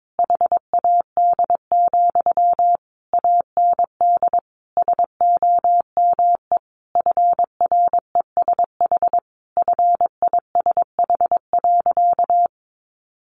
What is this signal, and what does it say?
Morse code, heard as a keyed tone, about 22 wpm, 700 Hz. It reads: HAD, AND SOME FRESH FISH.
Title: Sentences from Top 1500 Words 22wpm